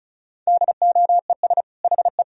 Morse code, heard as a keyed tone, about 36 wpm, 700 Hz. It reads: DOES HE
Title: Sentences from Top 1000 Words - Review Part 2 - 35wpm